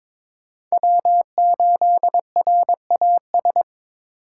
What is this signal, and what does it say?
Morse code, heard as a keyed tone, about 22 wpm, 700 Hz. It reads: W8RAS
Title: USA Call Signs - 4x Character Spacing - 22wpm